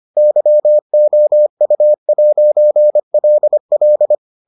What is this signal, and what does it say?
Morse code, heard as a keyed tone, about 25 wpm, 600 Hz. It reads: YOU'LL